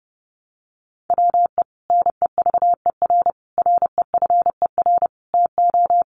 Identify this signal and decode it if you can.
Morse code, heard as a keyed tone, about 30 wpm, 700 Hz. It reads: WE NEVER REFER TO
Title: Sentences from Top 1000 Words 30wpm